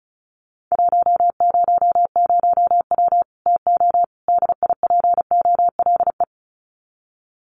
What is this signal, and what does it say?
Morse code, heard as a keyed tone, about 34 wpm, 700 Hz. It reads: 100W TO DIPOLE
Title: Common QSO Phrases 35wpm